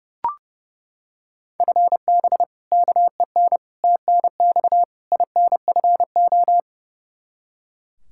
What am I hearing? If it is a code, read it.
Morse code, heard as a keyed tone, about 30 wpm, 700 Hz. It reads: FB KEN TNX INFO
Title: Practice QSO - Exchange 3 - More Info - 30wpm